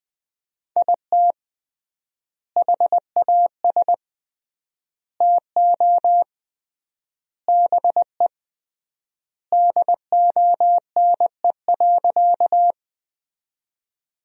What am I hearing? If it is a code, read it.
Morse code, heard as a keyed tone, about 20 wpm, 700 Hz. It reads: IT HAS TO BE DONE.
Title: 3x Word-Spacing - Sentences from Top 500 Words - 20wpm